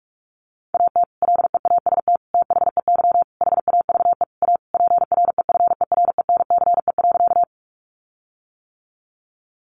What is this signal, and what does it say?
Morse code, heard as a keyed tone, about 44 wpm, 700 Hz. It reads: AT LEAST THEY HAVE A PREFERENCE.
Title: Sentences from Top 1500 Words 45wpm